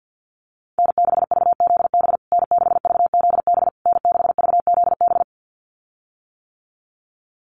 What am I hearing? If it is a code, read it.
Morse code, heard as a keyed tone, about 48 wpm, 700 Hz. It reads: N6VZB N6VZB N6VZB
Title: USA Call Signs Repeated 3x 50wpm